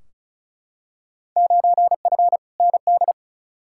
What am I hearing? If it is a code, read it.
Morse code, heard as a keyed tone, about 36 wpm, 700 Hz. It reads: 9F ND
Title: Field Day Exchanges Encore - 35wpm